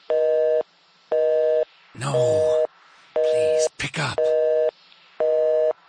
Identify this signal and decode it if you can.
Busy tone (North America), 480 + 620 Hz, with speech.
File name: phone numbers / calls / home 02 var2